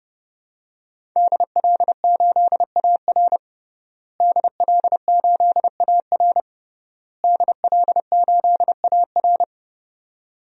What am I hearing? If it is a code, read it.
Morse code, heard as a keyed tone, about 30 wpm, 700 Hz. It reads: DL8AR DL8AR DL8AR
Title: DXCC-100 Call Signs - 5 Characters - Repeated 3x - 30wpm